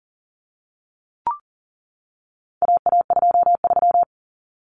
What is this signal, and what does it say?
Morse code, heard as a keyed tone, about 40 wpm, 700 Hz. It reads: AA23
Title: Common Cut Numbers - 4-Digit 40wpm